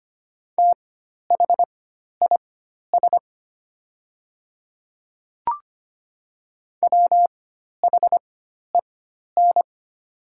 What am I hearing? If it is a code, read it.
Morse code, heard as a keyed tone, about 25 wpm, spaced as for 12 wpm, 700 Hz. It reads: THIS WHEN
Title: 4x Character Spacing - Top 100 Words - 25wpm